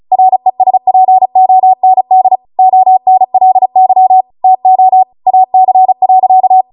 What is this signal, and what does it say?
Morse code, heard as a keyed tone, about 36 wpm, 750 Hz. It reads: RESPOND ONLY TO AC.